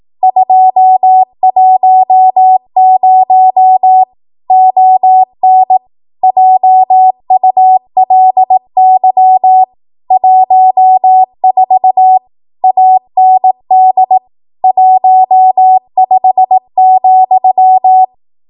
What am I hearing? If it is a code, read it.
Morse code, heard as a keyed tone, about 18 wpm, 750 Hz. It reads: 210 ON JULY 14 AND 15,
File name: Archive / 18 WPM / 18WPM